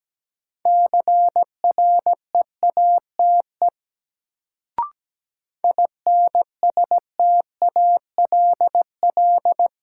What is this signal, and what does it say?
Morse code, heard as a keyed tone, about 17 wpm, 700 Hz. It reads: CREATE INSTALL
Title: Sentences from Top 1600 Words - New Word Drill Alt - 17wpm